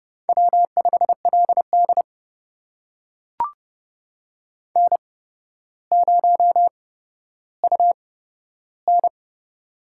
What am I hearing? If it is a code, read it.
Morse code, heard as a keyed tone, about 30 wpm, spaced as for 13 wpm, 700 Hz. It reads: W5LD N0UN
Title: USA Call Signs - 4 Characters - 8x Character Spacing - 30wpm